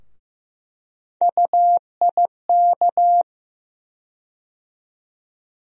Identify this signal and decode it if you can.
Morse code, heard as a keyed tone, about 15 wpm, 700 Hz. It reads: UIK
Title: USA Call Sign Postfixes - 3 Letters 15wpm